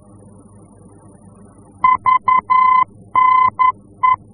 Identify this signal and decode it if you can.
Morse code, heard as a keyed tone, about 11 wpm, 1020 Hz. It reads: VNE
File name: france / mx / VNE-F-VOR